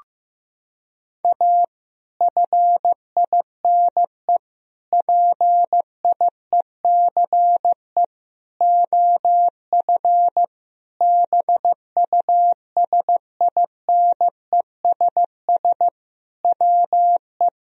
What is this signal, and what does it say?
Morse code, heard as a keyed tone, about 15 wpm, 700 Hz. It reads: A FINE PIECE OF BUSINESS WE
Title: Sentences from Top 1000 Words - Review Part 2 - 15wpm